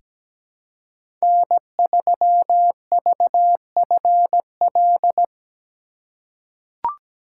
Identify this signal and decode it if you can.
Morse code, heard as a keyed tone, about 17 wpm, 700 Hz. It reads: N3VFL
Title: USA Call Signs - 5 Characters - 2x Character Spacing - 17wpm